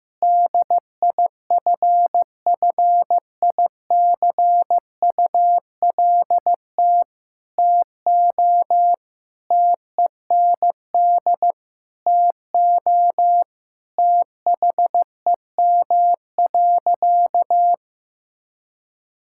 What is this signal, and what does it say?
Morse code, heard as a keyed tone, about 15 wpm, 700 Hz. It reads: DIFFICULT TO TEND TO THEM.
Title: Sentences from Top 1000 Words - Review Part 5 - 15wpm